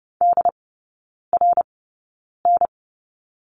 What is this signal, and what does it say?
Morse code, heard as a keyed tone, about 30 wpm, spaced as for 9 wpm, 700 Hz. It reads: DRN
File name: Lesson 17e - Letter u - Three Characters 30wpm